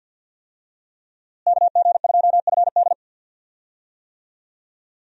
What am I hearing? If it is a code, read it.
Morse code, heard as a keyed tone, about 47 wpm, 700 Hz. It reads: KG2FD
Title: USA Call Signs - 5 Characters - Repeated 2x - 50wpm